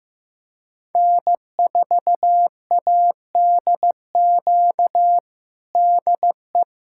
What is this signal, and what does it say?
Morse code, heard as a keyed tone, about 15 wpm, 700 Hz. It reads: N4ADQ DE
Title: USA Callsign DE Callsign - 5 Characters - 15wpm